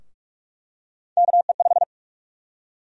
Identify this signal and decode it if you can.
Morse code, heard as a keyed tone, about 46 wpm, 700 Hz. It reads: KE5
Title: USA Call Sign Prefixes - Repeated 2x - 45wpm